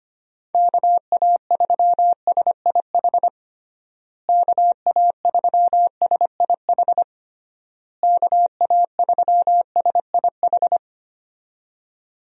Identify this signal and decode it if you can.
Morse code, heard as a keyed tone, about 25 wpm, 700 Hz. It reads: KA3SIH KA3SIH KA3SIH
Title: USA Call Signs - 6 Characters - Repeated 3x - 25wpm